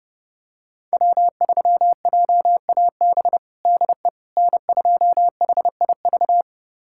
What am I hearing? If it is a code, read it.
Morse code, heard as a keyed tone, about 30 wpm, 700 Hz. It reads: W3JAB DE N2HIV
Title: USA Callsign DE Callsign - 5 Characters - 30wpm